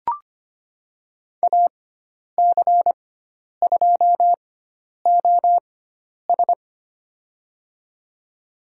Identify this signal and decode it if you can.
Morse code, heard as a keyed tone, about 25 wpm, spaced as for 10 wpm, 700 Hz. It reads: AC2OS
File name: Lesson 16g - Letter c - Call Signs 25wpm